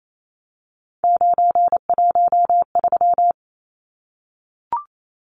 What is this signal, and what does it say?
Morse code, heard as a keyed tone, about 28 wpm, 700 Hz. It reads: 913
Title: Numbers - 3-Digit 28wpm